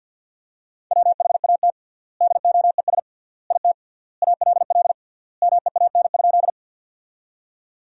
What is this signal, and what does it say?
Morse code, heard as a keyed tone, about 50 wpm, 700 Hz. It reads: WHAT DOES IT ALL MEAN?
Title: 3x Word-Spacing - Sentences from Top 200 Words - 50wpm